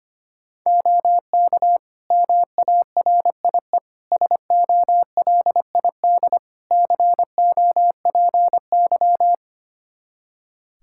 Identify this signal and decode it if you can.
Morse code, heard as a keyed tone, about 25 wpm, 700 Hz. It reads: OK MARIE SOLID COPY
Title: Practice QSO - Exchange 3 - More Info - 25wpm